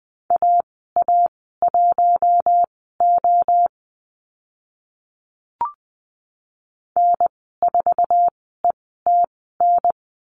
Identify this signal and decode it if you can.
Morse code, heard as a keyed tone, about 20 wpm, spaced as for 15 wpm, 700 Hz. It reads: AA1O N4ETN
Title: 048 Lesson 9g - Number 4 - Call Signs 20wpm